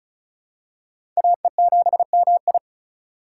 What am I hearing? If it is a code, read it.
Morse code, heard as a keyed tone, about 36 wpm, 700 Hz. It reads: AE7MI
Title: USA Call Signs - 5 Characters - 4x Character Spacing - 35wpm